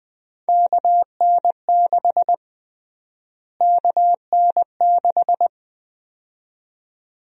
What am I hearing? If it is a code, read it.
Morse code, heard as a keyed tone, about 20 wpm, 700 Hz. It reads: KN6 KN6
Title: USA Call Sign Prefixes - Repeated 2x - 20wpm